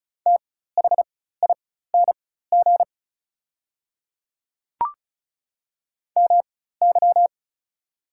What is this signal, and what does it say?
Morse code, heard as a keyed tone, about 33 wpm, spaced as for 16 wpm, 690 Hz. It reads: THING MY
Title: 4x Character Spacing - Top 100 Words - 35wpm